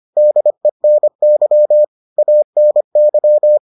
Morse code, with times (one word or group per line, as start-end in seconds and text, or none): 0.2-1.8 DENY
2.2-3.6 ANY